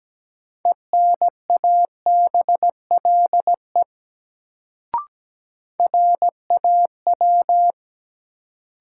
Morse code, heard as a keyed tone, about 17 wpm, 700 Hz. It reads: ENABLE RAW
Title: Sentences from Top 1500 Words - New Word Drill Alt - 17wpm